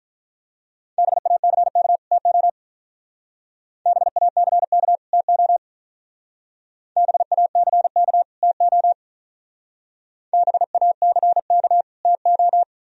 Morse code, heard as a keyed tone, about 43 wpm, 700 Hz. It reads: BACK TO BACK TO BACK TO BACK TO
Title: Speed-Racing - Sets of 2 Words 35wpm